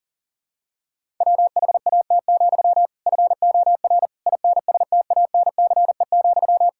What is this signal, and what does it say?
Morse code, heard as a keyed tone, about 40 wpm, 700 Hz. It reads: WHAT, FOR INSTANCE,
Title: Sentences from Top 1000 Words - Review Part 5 - 40wpm